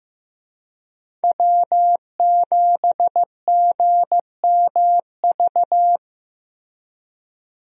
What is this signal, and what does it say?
Morse code, heard as a keyed tone, about 15 wpm, 700 Hz. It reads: W7GMV
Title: USA Call Signs - 5 Characters - Repeated 2x - 15wpm